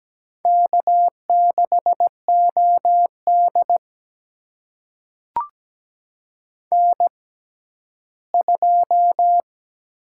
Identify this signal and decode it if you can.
Morse code, heard as a keyed tone, about 17 wpm, spaced as for 12 wpm, 700 Hz. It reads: K6OD N2
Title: USA Call Signs - 4 Characters - 6x Character Spacing - 17wpm